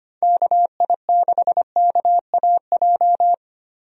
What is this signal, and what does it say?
Morse code, heard as a keyed tone, about 25 wpm, 700 Hz. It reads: KI6KAJ
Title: USA Call Signs - 6 Characters - 25wpm